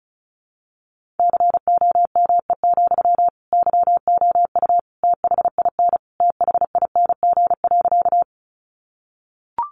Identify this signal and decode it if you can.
Morse code, heard as a keyed tone, about 34 wpm, 700 Hz. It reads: COME, YOU THIN THING.
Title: Sentences from Top 1200 Words 35wpm